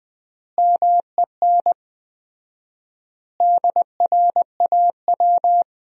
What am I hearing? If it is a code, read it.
Morse code, heard as a keyed tone, about 20 wpm, 700 Hz. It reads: MEN DRAW
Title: 4x Word-Spacing - Sentences from Top 300 Words - 20wpm